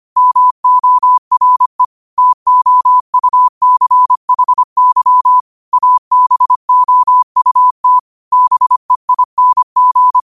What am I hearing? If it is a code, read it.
Morse code, heard as a keyed tone, about 25 wpm, 1000 Hz. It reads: MORE TOUCHY ABOUT BEING
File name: qotdfiles / 25wpm